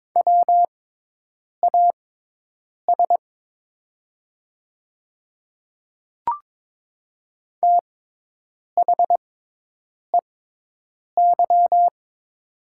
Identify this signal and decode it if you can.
Morse code, heard as a keyed tone, about 22 wpm, spaced as for 8 wpm, 690 Hz. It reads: WAS THEY
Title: 6x Character Spacing - Top 100 Words - 22wpm